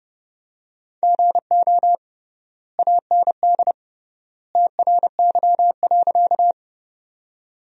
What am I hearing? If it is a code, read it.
Morse code, heard as a keyed tone, about 30 wpm, 700 Hz. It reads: GO AND TRY.
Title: 3x Word-Spacing - Sentences from Top 200 Words - 30wpm